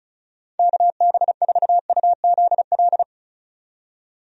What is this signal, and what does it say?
Morse code, heard as a keyed tone, about 34 wpm, 700 Hz. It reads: KB4UZL
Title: USA Call Signs - 6 Characters - 35wpm